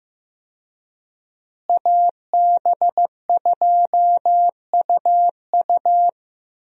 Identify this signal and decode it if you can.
Morse code, heard as a keyed tone, about 15 wpm, 700 Hz. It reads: AB2UU
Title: USA Call Signs - 5 Characters - Repeated 3x - 15wpm